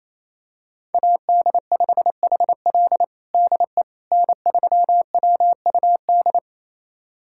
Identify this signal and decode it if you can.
Morse code, heard as a keyed tone, about 28 wpm, 700 Hz. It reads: AD5HL DE N3WUD
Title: USA Callsign DE Callsign - 5 Characters - 28wpm